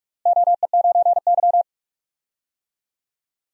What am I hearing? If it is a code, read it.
Morse code, heard as a keyed tone, about 46 wpm, 700 Hz. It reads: OE9Y
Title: DXCC-100 Call Signs - 4 Characters - Repeated 2x - 45wpm